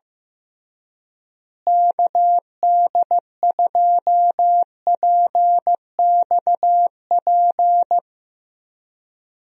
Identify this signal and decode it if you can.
Morse code, heard as a keyed tone, about 15 wpm, 700 Hz. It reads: KD2PXP
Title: USA Call Signs - 6 Characters - Repeated 3x - 15wpm